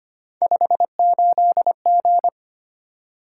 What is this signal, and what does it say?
Morse code, heard as a keyed tone, about 25 wpm, 700 Hz. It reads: 58G
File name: Three Letter-Number Combinations - 25wpm